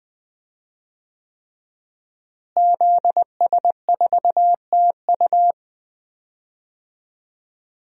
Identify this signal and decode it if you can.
Morse code, heard as a keyed tone, about 20 wpm, 700 Hz. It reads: ZS4TU
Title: DXCC-100 Call Signs - 5 Characters - Repeated 3x - 20wpm